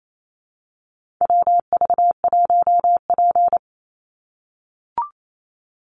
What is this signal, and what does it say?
Morse code, heard as a keyed tone, about 28 wpm, 700 Hz. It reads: WV1P
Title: USA Call Signs 28wpm